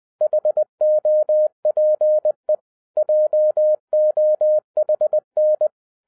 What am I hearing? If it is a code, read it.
Morse code, heard as a keyed tone, about 20 wpm, 610 Hz. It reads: HOPE JOHN
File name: 2011-07-mp3 / n48 JOHN K3TN 20wpm